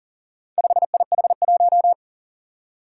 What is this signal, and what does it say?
Morse code, heard as a keyed tone, about 40 wpm, 700 Hz. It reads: 5IH1
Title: Four Challenging Letter-Number Combinations 40wpm